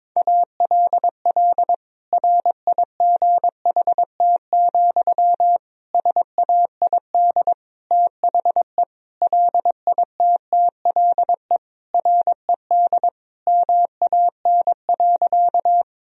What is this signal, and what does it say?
Morse code, heard as a keyed tone, about 22 wpm, 700 Hz. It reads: ALL RIGHT, SAID THE LITTLE RED MAN.
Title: Sentences from Top 1000 Words - Review Part 2 - 22wpm